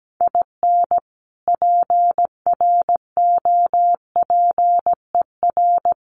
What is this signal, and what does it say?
Morse code, heard as a keyed tone, about 17 wpm, 700 Hz. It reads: IN PROPER
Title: Sentences from Top 1400 Words 17wpm